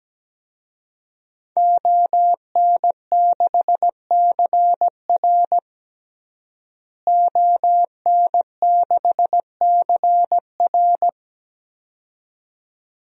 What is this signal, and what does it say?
Morse code, heard as a keyed tone, about 17 wpm, 700 Hz. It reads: ON6CR ON6CR
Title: DXCC-100 Call Signs - 5 Characters - Repeated 2x - 17wpm